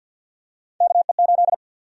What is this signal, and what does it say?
Morse code, heard as a keyed tone, about 50 wpm, 700 Hz. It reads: KE8
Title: USA Call Sign Prefixes - 50wpm